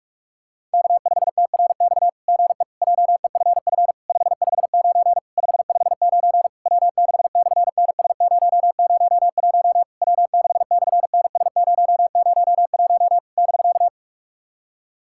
Morse code, heard as a keyed tone, about 46 wpm, 690 Hz. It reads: K5TRX GE JEFF 559 559 W6/NS001 W6/NS001 <BK>
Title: SOTA QSO Practice 45wpm